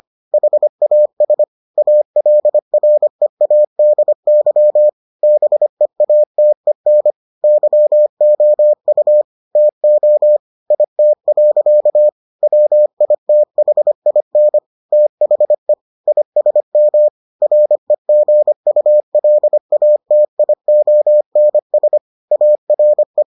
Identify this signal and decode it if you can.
Morse code, heard as a keyed tone, about 25 wpm, 600 Hz. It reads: HAS ALREADY BEATEN YOU TO IT. WITHIN THE ISM REGULATIONS ARE